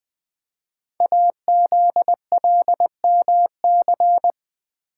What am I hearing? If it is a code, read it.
Morse code, heard as a keyed tone, about 20 wpm, 700 Hz. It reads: AZLMC
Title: Code Groups - 5 Letters - 20wpm